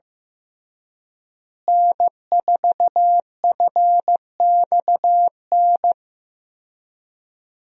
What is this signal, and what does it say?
Morse code, heard as a keyed tone, about 15 wpm, 700 Hz. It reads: N4FXN
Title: USA Call Signs - 5 Characters - Repeated 2x - 15wpm